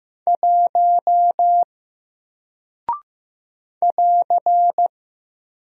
Morse code, heard as a keyed tone, about 15 wpm, 700 Hz. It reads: 1 <AR>
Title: Single Character (Letter:Number:Punctuation:Prosign) Alt 15wpm